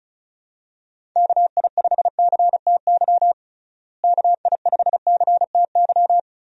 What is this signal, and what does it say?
Morse code, heard as a keyed tone, about 36 wpm, 700 Hz. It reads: KI5CTY KI5CTY
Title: USA Call Signs - 6 Characters - Repeated 3x - 35wpm